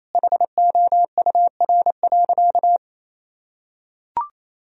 Morse code, heard as a keyed tone, about 28 wpm, 700 Hz. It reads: HOUR.
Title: Sentences from Top 1000 Words - Review Part 4 - 28wpm